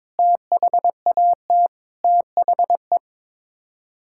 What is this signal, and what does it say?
Morse code, heard as a keyed tone, about 22 wpm, 700 Hz. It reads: THAT THE
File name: practice / 22wpm / Sets of 7 Words - 22wpm